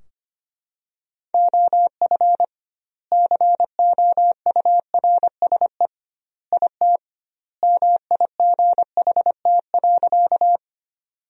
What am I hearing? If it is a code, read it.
Morse code, heard as a keyed tone, about 25 wpm, 700 Hz. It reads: OF COURSE IT MIGHT.
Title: 2x Word-Spacing - Sentences from Top 500 Words - 25wpm